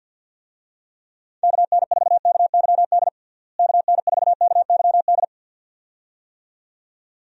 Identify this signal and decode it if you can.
Morse code, heard as a keyed tone, about 49 wpm, 700 Hz. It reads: KN4KYD KN4KYD
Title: USA Call Signs - 6 Characters - Repeated 2x - 50wpm